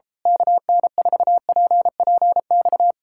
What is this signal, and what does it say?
Morse code, heard as a keyed tone, about 32 wpm, 700 Hz. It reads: KN4PPX
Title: Speed-Racing Common QSO Phrases - 28wpm